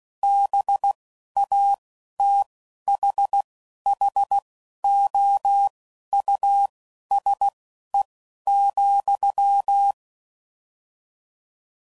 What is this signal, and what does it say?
Morse code, heard as a keyed tone, about 16 wpm, spaced as for 11 wpm, 800 Hz. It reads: BATHHOUSE,